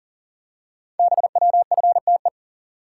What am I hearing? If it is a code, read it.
Morse code, heard as a keyed tone, about 38 wpm, 700 Hz. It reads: BWFTE
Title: Code Groups - 5 Letters - 40wpm